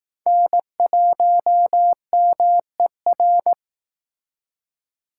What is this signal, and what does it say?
Morse code, heard as a keyed tone, about 19 wpm, 700 Hz. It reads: N1MER